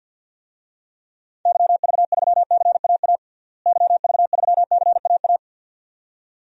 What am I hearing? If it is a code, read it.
Morse code, heard as a keyed tone, about 46 wpm, 700 Hz. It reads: YV3CAA YV3CAA
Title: DXCC-100 Call Signs - 6 Characters - Repeated 2x - 50wpm